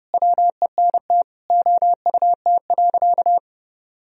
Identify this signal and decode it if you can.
Morse code, heard as a keyed tone, about 30 wpm, 700 Hz. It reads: WENT OUT.
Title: Sentences from Top 1000 Words - Review Part 1 - 30wpm